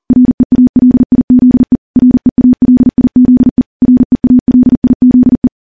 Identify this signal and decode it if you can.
Morse code, heard as a keyed tone, about 40 wpm, 250 Hz. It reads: REALIZE REALIZE REALIZE
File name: morse / morse-training-40WPM